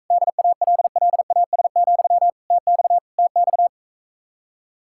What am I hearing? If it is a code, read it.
Morse code, heard as a keyed tone, about 45 wpm, 700 Hz. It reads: DALLAS, TX TX